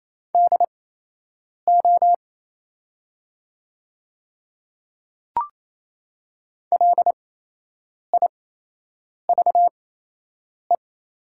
Morse code, heard as a keyed tone, about 28 wpm, spaced as for 8 wpm, 700 Hz. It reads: DO LIVE
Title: 8x Character Spacing - Top 200 Words - 28wpm